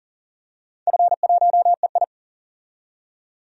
Morse code, heard as a keyed tone, about 37 wpm, 700 Hz. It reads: F1EI
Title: DXCC-100 Call Signs - 4 Characters - 40wpm